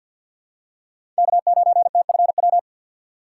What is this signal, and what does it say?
Morse code, heard as a keyed tone, about 53 wpm, 700 Hz. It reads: K9TFW
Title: USA Call Signs - 6x Character Spacing - 50wpm